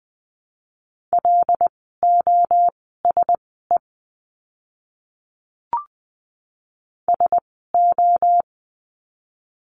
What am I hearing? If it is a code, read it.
Morse code, heard as a keyed tone, about 20 wpm, spaced as for 15 wpm, 700 Hz. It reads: LOSE SO
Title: 101 Lesson 17f - Letter u - Words 20wpm